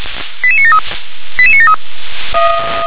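Five-tone selective call (ZVEI), heard as quick runs of five tones, tones of about 80 ms. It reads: 88973 88973
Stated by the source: ZVEI five-tone call received by SDR.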